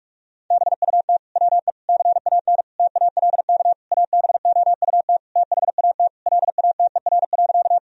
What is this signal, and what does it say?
Morse code, heard as a keyed tone, about 41 wpm, 700 Hz. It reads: BUT WE CAN TALK ABOUT THAT LATER.
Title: Sentences from Top 1000 Words - Review Part 4 - 45wpm